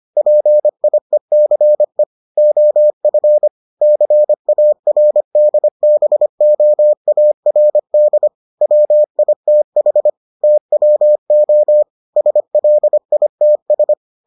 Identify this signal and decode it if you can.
Morse code, heard as a keyed tone, about 25 wpm, 600 Hz. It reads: PIECE OF CARDBOARD WITH TWO SLITS